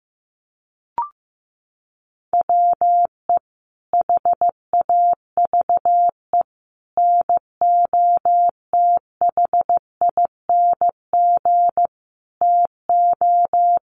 Morse code, heard as a keyed tone, about 15 wpm, 700 Hz. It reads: WE HAVE NOTHING TO
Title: Sentences from Top 1100 Words 15wpm